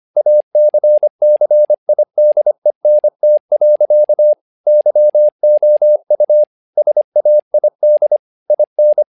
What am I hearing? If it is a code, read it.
Morse code, heard as a keyed tone, about 25 wpm, 600 Hz. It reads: ACCIDENT. YOU SAID IN
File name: learning center / tower trouble / tower trouble 25wpm